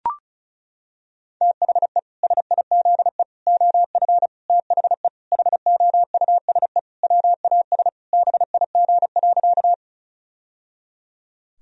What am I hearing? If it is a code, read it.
Morse code, heard as a keyed tone, about 34 wpm, 700 Hz. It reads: THE SIZE OF THE HOUSE WAS BIG.
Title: Sentences from Top 1000 Words - Review Part 2 - 35wpm